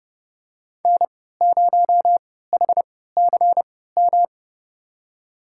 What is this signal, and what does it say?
Morse code, heard as a keyed tone, about 30 wpm, spaced as for 17 wpm, 700 Hz. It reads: N0HCM
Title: USA Call Signs - 5 Characters - 3x Character Spacing - 30wpm